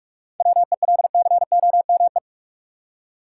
Morse code, heard as a keyed tone, about 47 wpm, 700 Hz. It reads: WELCOME
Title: Seven Letter Words - 45wpm